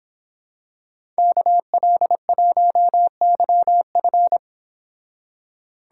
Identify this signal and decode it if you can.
Morse code, heard as a keyed tone, about 26 wpm, 700 Hz. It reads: KL1YF